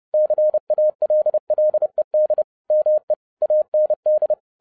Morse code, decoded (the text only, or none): CALLED ME AND